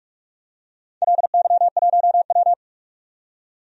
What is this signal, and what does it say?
Morse code, heard as a keyed tone, about 46 wpm, 700 Hz. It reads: LY1W
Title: DXCC-100 Call Signs - 4 Characters - 45wpm